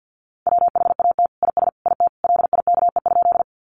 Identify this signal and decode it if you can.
Morse code, heard as a keyed tone, about 50 wpm, 700 Hz. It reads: WHAT IS IT LIKE?
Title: Sentences from Top 100 Words 50wpm